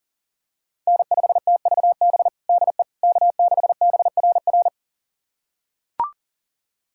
Morse code, heard as a keyed tone, about 40 wpm, 700 Hz. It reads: N5TVB DE K6BRR
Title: USA Callsign DE Callsign - 5 Characters - 40wpm